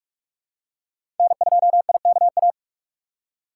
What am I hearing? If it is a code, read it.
Morse code, heard as a keyed tone, about 46 wpm, 700 Hz. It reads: N2IKA
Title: USA Call Signs - 5 Characters - 4x Character Spacing - 45wpm